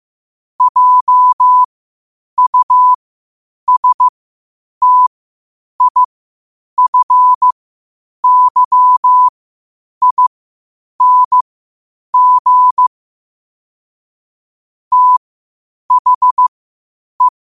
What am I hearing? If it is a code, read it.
Morse code, heard as a keyed tone, about 15 wpm, spaced as for 8 wpm, 1000 Hz. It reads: JUSTIFYING THE
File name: qotdfiles / 8wpm